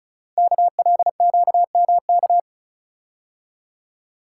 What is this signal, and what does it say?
Morse code, heard as a keyed tone, about 33 wpm, 700 Hz. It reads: KLQMK
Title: Code Groups - 5 Letters - 35wpm